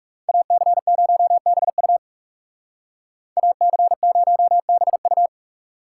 Morse code, heard as a keyed tone, about 39 wpm, 700 Hz. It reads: AC0BU AC0BU